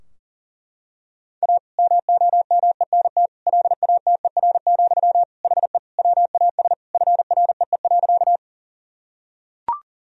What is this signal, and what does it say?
Morse code, heard as a keyed tone, about 37 wpm, 700 Hz. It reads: A MOMENT LATER, HE WAS FREE.
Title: Sentences from Top 1000 Words - Review Part 4 - 40wpm